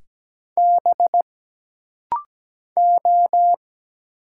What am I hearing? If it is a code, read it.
Morse code, heard as a keyed tone, about 17 wpm, 700 Hz. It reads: B O
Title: Single Letters - Rapid-Fire 17wpm